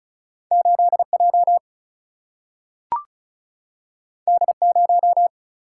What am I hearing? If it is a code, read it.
Morse code, heard as a keyed tone, about 36 wpm, 700 Hz. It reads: 8 J D 0
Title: Single-Letter Single-Number 35wpm